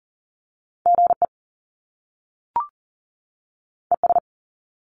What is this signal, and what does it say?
Morse code, heard as a keyed tone, about 40 wpm, 700 Hz. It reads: GE ES
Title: Top Two Letter Combinations in Words 40wpm